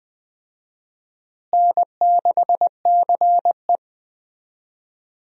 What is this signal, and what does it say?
Morse code, heard as a keyed tone, about 20 wpm, 700 Hz. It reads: N6CE
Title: USA Call Signs - 4 Characters - Repeated 3x - 20wpm